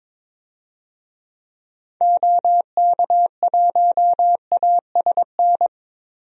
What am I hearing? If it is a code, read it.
Morse code, heard as a keyed tone, about 22 wpm, 700 Hz. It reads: OK1ASN
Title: DXCC-100 Call Signs - 6 Characters - Repeated 3x - 22wpm